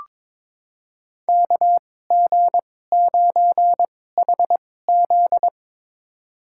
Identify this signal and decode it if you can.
Morse code, heard as a keyed tone, about 22 wpm, spaced as for 16 wpm, 700 Hz. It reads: KG9HZ
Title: USA Call Signs - 2x Character Spacing - 22wpm